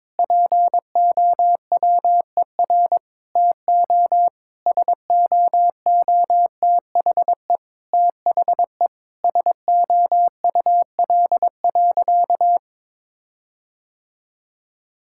Morse code, heard as a keyed tone, about 21 wpm, 700 Hz. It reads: POWER TO SOOTHE THE SOUL.
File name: Difficult Sentences with Hints - 22wpm